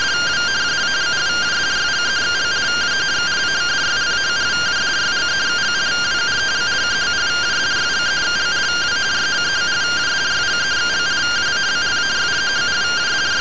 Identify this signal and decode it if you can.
Two-tone teleprinter signal, about 170 Hz shift.